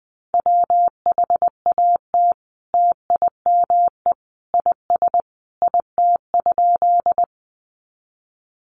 Morse code, heard as a keyed tone, about 20 wpm, 700 Hz. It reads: WHAT TIME IS IT?
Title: Sentences from Top 100 Words 20wpm